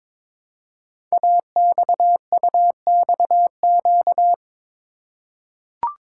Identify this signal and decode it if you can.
Morse code, heard as a keyed tone, about 22 wpm, 700 Hz. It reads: AXUXQ
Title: Code Groups - 5 Letters - 22wpm